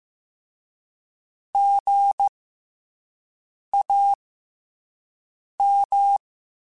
Morse code, heard as a keyed tone, about 15 wpm, spaced as for 5 wpm, 800 Hz. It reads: GAM
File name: Special INGs-mp3s / Special INGs-mp3s / SlowToFaster ing @10